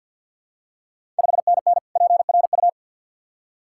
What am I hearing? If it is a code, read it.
Morse code, heard as a keyed tone, about 52 wpm, 700 Hz. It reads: 5NN PRU
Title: Ontario QSO Party Exchange 50wpm